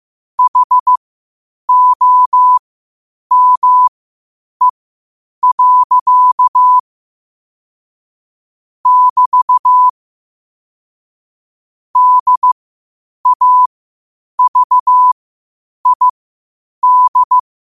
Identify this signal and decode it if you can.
Morse code, heard as a keyed tone, about 15 wpm, spaced as for 8 wpm, 1000 Hz. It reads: HOME. <BT> DAVID
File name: qotdfiles / 8wpm